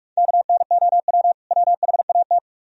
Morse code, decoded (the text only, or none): KNOW WHAT